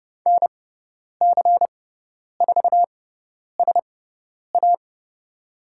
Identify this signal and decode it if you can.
Morse code, heard as a keyed tone, about 30 wpm, spaced as for 10 wpm, 690 Hz. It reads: NC4SA
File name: Lesson 18g - Letter m - Call Signs 30wpm